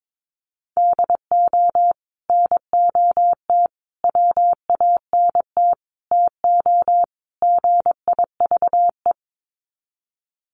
Morse code, heard as a keyed tone, about 22 wpm, 700 Hz. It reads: DO NOT WANT TO GIVE
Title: Sets of 5 Words Encore 22wpm